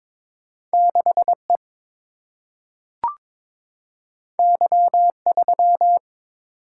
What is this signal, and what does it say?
Morse code, heard as a keyed tone, about 22 wpm, 700 Hz. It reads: 6 E Y 3
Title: Single-Letter Single-Number 22wpm